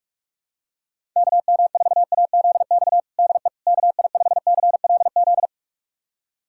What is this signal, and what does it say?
Morse code, heard as a keyed tone, about 41 wpm, 700 Hz. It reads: KM4AZX DE KI5CLZ
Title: USA Callsign DE Callsign - 6 Characters - 45wpm